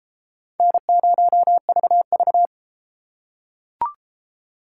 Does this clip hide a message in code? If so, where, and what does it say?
Yes: Morse code from 0.6 to 2.5 s, heard as a keyed tone, about 33 wpm, 700 Hz. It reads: N0VV